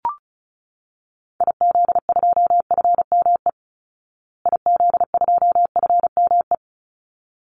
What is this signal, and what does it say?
Morse code, heard as a keyed tone, about 34 wpm, 700 Hz. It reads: IZ2FME IZ2FME
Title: Call Signs Repeated 3x - 35wpm